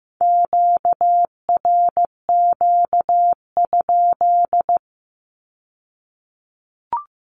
Morse code, heard as a keyed tone, about 15 wpm, 700 Hz. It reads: QRQ?
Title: Q-Codes with Spoken Meaning - 15wpm